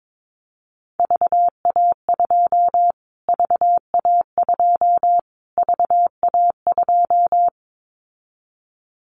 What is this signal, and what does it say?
Morse code, heard as a keyed tone, about 22 wpm, 700 Hz. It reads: VA2 VA2 VA2
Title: Callsign Prefixes for Top 25 DXCC - Repeated 3x - 22wpm